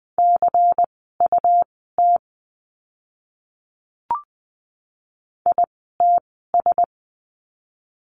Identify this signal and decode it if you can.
Morse code, heard as a keyed tone, about 20 wpm, spaced as for 15 wpm, 700 Hz. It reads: CUT ITS
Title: 101 Lesson 17f - Letter u - Words 20wpm